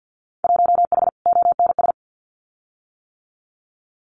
Morse code, heard as a keyed tone, about 50 wpm, 700 Hz. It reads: 1H ONS
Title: Winter Field Day Exchanges 50wpm